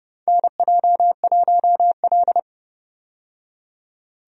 Morse code, heard as a keyed tone, about 30 wpm, 700 Hz. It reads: NJ1L
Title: USA Call Signs - 4 Characters - 30wpm